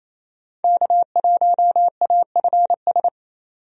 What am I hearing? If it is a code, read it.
Morse code, heard as a keyed tone, about 28 wpm, 700 Hz. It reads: K1AFS
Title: USA Call Signs - 5 Characters - 28wpm